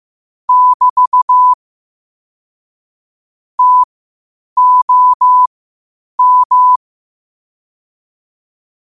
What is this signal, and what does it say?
Morse code, heard as a keyed tone, about 15 wpm, spaced as for 8 wpm, 1000 Hz. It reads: <BT> TOM
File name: 8wpm